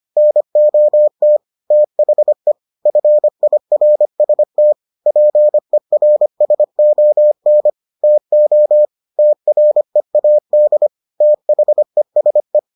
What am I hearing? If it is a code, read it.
Morse code, heard as a keyed tone, about 25 wpm, 600 Hz. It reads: NOT THE FIRST PERSON TO TREAD THESE